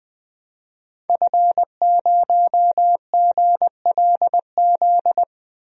Morse code, heard as a keyed tone, about 20 wpm, 700 Hz. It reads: F0GLZ
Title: DXCC-100 Call Signs - 5 Characters - 20wpm